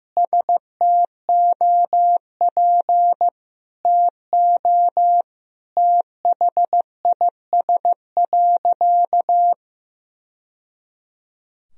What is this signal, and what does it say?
Morse code, heard as a keyed tone, about 15 wpm, 700 Hz. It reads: STOP TO THIS.
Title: Sentences from Top 1000 Words - Review Part 1 - 15wpm